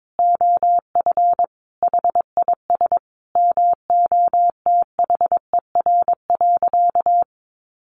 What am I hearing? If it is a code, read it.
Morse code, heard as a keyed tone, about 22 wpm, 700 Hz. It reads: OF HIS MOTHER.
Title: Sentences from Top 1300 Words 22wpm